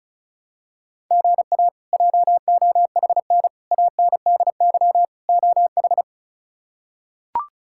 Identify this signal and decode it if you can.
Morse code, heard as a keyed tone, about 36 wpm, 700 Hz. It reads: GA JOHN ANDY OH
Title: K1USN SST QSO Practice - 35wpm